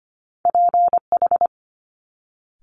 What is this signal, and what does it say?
Morse code, heard as a keyed tone, about 25 wpm, 700 Hz. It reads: PH
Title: Multi-letter Phonemes Quick Fire 25wpm